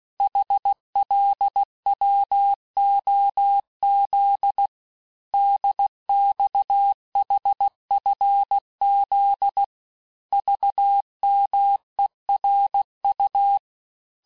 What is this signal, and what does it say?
Morse code, heard as a keyed tone, about 16 wpm, 800 Hz. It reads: HLWOZ DXHFZ VMERU
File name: cours-du-lundi files / cours-1592--teste-chiffre-vitesse-15mpm